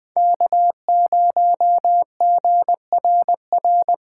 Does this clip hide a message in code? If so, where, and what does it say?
Yes: Morse code from 0.2 to 3.9 s, heard as a keyed tone, about 21 wpm, 700 Hz. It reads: K0GRR